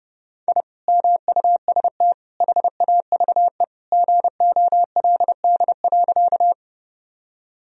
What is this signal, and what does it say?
Morse code, heard as a keyed tone, about 30 wpm, 700 Hz. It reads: I MUST HAVE GOLD.
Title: Sentences from Top 1000 Words - Review Part 3 - 30wpm